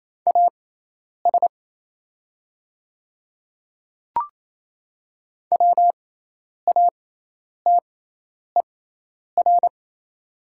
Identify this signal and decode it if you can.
Morse code, heard as a keyed tone, about 28 wpm, spaced as for 10 wpm, 700 Hz. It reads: AS WATER
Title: 6x Character Spacing - Top 100 Words - 28wpm